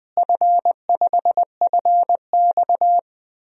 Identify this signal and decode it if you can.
Morse code, heard as a keyed tone, about 20 wpm, 700 Hz. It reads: F5FX
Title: DXCC-100 Call Signs - 4 Characters - Repeated 3x - 20wpm